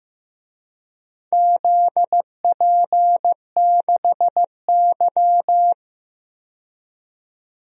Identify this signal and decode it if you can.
Morse code, heard as a keyed tone, about 15 wpm, 700 Hz. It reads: ZP6Y
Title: DXCC-100 Call Signs - 4 Characters - 15wpm